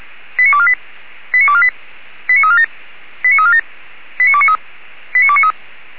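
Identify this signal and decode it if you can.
Five-tone selective call (ZVEI), heard as quick runs of five tones, tones of about 70 ms. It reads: 89248 89248 89347 89347 89293 89293
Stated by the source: ZVEI five-tone call 89406, received by SDR.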